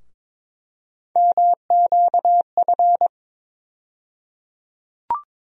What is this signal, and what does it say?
Morse code, heard as a keyed tone, about 22 wpm, 700 Hz. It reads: MQF
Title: USA Call Sign Postfixes - 3 Letters 22wpm